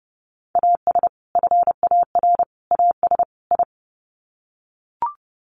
Morse code, heard as a keyed tone, about 29 wpm, 700 Hz. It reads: AS FAR AS I
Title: Sets of 4 Words 30wpm